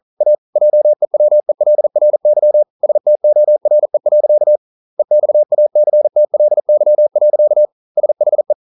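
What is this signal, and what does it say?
Morse code, heard as a keyed tone, about 42 wpm, 600 Hz. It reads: A JEWELRY STORE. EXACTLY. SHE
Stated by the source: Morse code practice at 41 wpm, from CW med Gnister.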